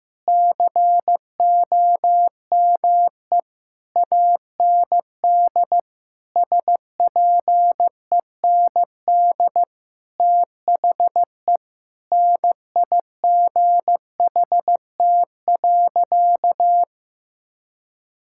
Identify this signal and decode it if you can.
Morse code, heard as a keyed tone, about 15 wpm, 700 Hz. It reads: COME AND SPEND THE NIGHT.
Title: Sentences from Top 1000 Words - Review Part 5 - 15wpm